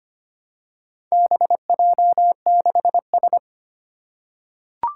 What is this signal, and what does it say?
Morse code, heard as a keyed tone, about 25 wpm, 700 Hz. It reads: BJ6S
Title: Four Challenging Letter-Number Combinations 25wpm